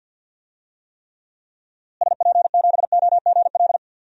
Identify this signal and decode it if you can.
Morse code, heard as a keyed tone, about 53 wpm, 700 Hz. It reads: SP7OGL
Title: DXCC-100 Call Signs - 6 Characters - Repeated 2x - 50wpm